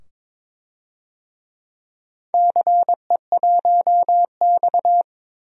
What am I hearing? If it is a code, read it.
Morse code, heard as a keyed tone, about 22 wpm, 700 Hz. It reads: CE1X
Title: DXCC-100 Call Signs - 4 Characters - 22wpm